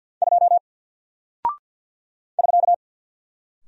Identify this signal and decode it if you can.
Morse code, heard as a keyed tone, about 51 wpm, 700 Hz. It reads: 2 <SK>
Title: Single Character (Letter:Number:Punctuation:Prosign) Alt 50wpm